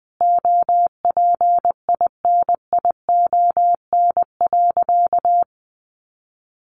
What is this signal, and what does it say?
Morse code, heard as a keyed tone, about 20 wpm, 700 Hz. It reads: OPINION.
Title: Sentences from Top 1200 Words 20wpm